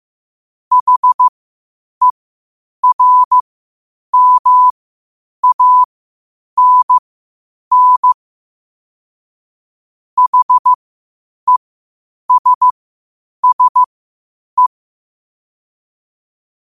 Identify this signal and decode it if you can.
Morse code, heard as a keyed tone, about 15 wpm, spaced as for 8 wpm, 1000 Hz. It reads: HERMANN HESSE
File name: qotdfiles / 8wpm